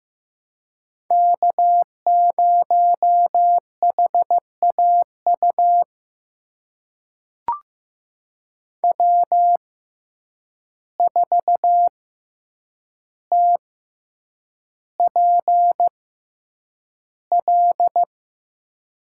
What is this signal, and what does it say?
Morse code, heard as a keyed tone, about 16 wpm, spaced as for 8 wpm, 700 Hz. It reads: K0HAU W4TPL
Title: USA Call Signs - 5 Characters - 6x Character Spacing - 15wpm